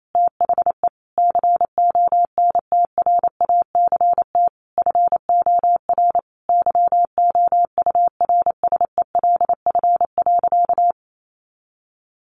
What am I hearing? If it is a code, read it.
Morse code, heard as a keyed tone, about 28 wpm, 700 Hz. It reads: THE CONTRACT FOR YOURSELF.
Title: Sentences from Top 1400 Words 28wpm